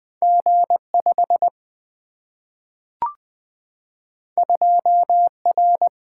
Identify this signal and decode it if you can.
Morse code, heard as a keyed tone, about 20 wpm, 700 Hz. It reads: G 5 2 R
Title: Single-Letter Single-Number 20wpm